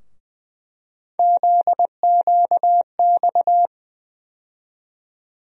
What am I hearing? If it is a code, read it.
Morse code, heard as a keyed tone, about 20 wpm, 700 Hz. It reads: ZQX
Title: USA Call Sign Postfixes - 3 Letters 20wpm